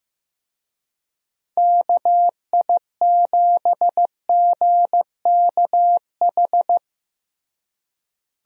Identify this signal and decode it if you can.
Morse code, heard as a keyed tone, about 15 wpm, 700 Hz. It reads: KI7GKH
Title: USA Call Signs - 6 Characters - Repeated 2x - 15wpm